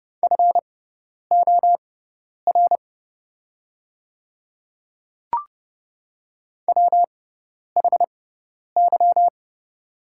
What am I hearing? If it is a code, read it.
Morse code, heard as a keyed tone, about 30 wpm, spaced as for 10 wpm, 690 Hz. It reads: FOR WHY
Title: 6x Character Spacing - Top 200 Words - 30wpm